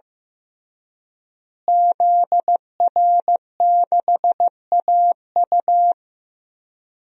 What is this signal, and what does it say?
Morse code, heard as a keyed tone, about 15 wpm, 700 Hz. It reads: ZR6AU
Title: DXCC-100 Call Signs - 5 Characters - Repeated 3x - 15wpm